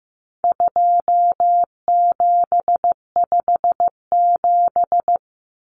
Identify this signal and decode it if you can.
Morse code, heard as a keyed tone, about 15 wpm, 700 Hz. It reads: 2757
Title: Common Cut Numbers - 4-Digit 15wpm